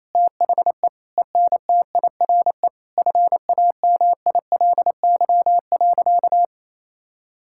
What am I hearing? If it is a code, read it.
Morse code, heard as a keyed tone, about 27 wpm, 700 Hz. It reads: THE ENTIRE FAMILY.
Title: Sentences from Top 1000 Words - Review Part 5 - 28wpm